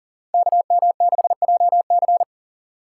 Morse code, heard as a keyed tone, about 39 wpm, 690 Hz. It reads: KM6JC
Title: Code Groups - 5 Letter:Numbers - 40wpm